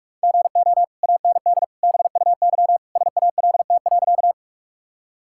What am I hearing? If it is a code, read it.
Morse code, heard as a keyed tone, about 41 wpm, 700 Hz. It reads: GO AND BUY SALT.
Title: Sentences from Top 1000 Words - Review Part 5 - 45wpm